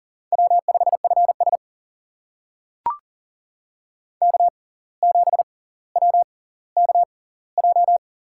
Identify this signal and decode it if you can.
Morse code, heard as a keyed tone, about 40 wpm, spaced as for 19 wpm, 700 Hz. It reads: W5FS K7WKJ
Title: USA Call Signs - 6x Character Spacing - 40wpm